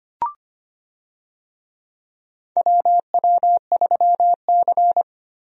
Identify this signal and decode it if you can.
Morse code, heard as a keyed tone, about 25 wpm, 700 Hz. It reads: WW3C
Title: USA Call Signs - 4 Characters - Repeated 3x - 25wpm